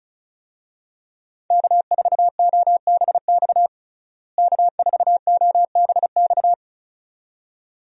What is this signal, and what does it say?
Morse code, heard as a keyed tone, about 36 wpm, 700 Hz. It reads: K4OBX K4OBX
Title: USA Call Signs - 5 Characters - Repeated 2x - 35wpm